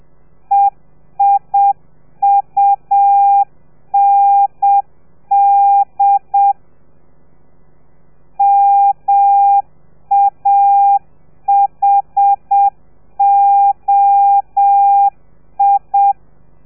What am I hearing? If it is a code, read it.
Morse code, heard as a keyed tone, about 7 wpm, 800 Hz. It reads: EIUND MAHOI